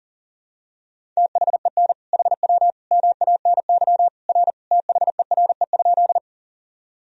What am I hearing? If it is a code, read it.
Morse code, heard as a keyed tone, about 37 wpm, 700 Hz. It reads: THEN HW MANY R THERE?
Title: Sentences from Top 100 Words and Abbreviations 40wpm